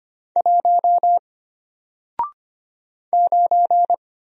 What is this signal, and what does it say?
Morse code, heard as a keyed tone, about 25 wpm, 700 Hz. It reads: 1 9
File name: Single-Letter-Number-25wpm